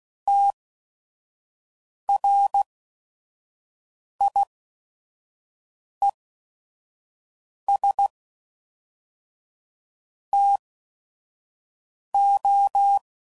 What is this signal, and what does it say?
Morse code, heard as a keyed tone, about 16 wpm, spaced as for 5 wpm, 800 Hz. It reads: TRIES TO
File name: CW Code Groups @5WPM / Technical Subjects / XC